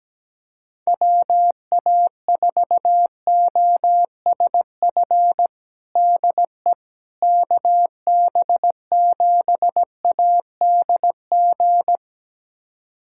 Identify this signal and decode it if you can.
Morse code, heard as a keyed tone, about 17 wpm, 700 Hz. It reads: WA4OSF DE KB7ADG
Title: USA Callsign DE Callsign - 6 Characters - 17wpm